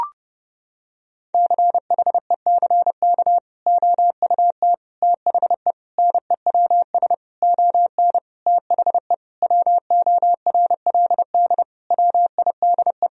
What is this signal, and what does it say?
Morse code, heard as a keyed tone, about 32 wpm, 700 Hz. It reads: CHECK OUT THE NEWS ON THE WORLD WIDE
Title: Sentences from Top 1000 Words - Review Part 4 - 30wpm